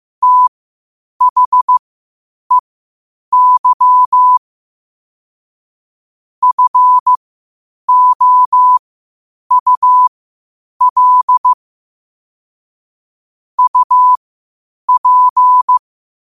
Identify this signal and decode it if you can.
Morse code, heard as a keyed tone, about 15 wpm, spaced as for 8 wpm, 1000 Hz. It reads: THEY FOUL UP
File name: qotdfiles / 8wpm